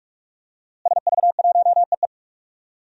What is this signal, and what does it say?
Morse code, heard as a keyed tone, about 41 wpm, 700 Hz. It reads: SV1EE
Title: DXCC-100 Call Signs - 5 Characters - Repeated 2x - 45wpm